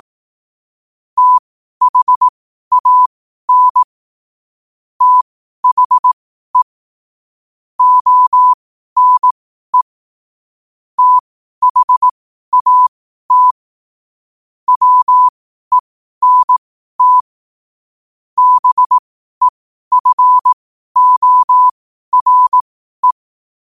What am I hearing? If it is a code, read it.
Morse code, heard as a keyed tone, about 18 wpm, spaced as for 12 wpm, 1000 Hz. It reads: THAN THE ONE THAT WENT BEFORE
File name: qotdfiles / 12wpm